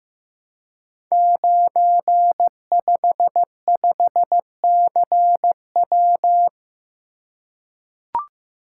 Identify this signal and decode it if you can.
Morse code, heard as a keyed tone, about 15 wpm, 700 Hz. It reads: 955CW
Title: Code Groups - 5 Letter:Numbers - 15wpm